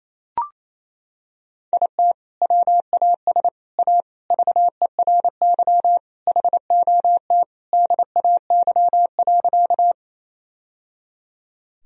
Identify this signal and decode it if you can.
Morse code, heard as a keyed tone, about 27 wpm, 700 Hz. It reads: IT WAS A VERY HOT DAY.
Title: Sentences from Top 1000 Words - Review Part 3 - 28wpm